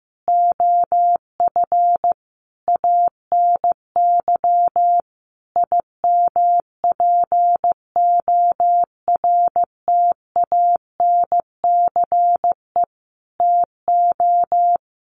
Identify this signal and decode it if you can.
Morse code, heard as a keyed tone, about 15 wpm, 700 Hz. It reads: OF ANY IMPORTANCE TO
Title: Sentences from Top 1200 Words 15wpm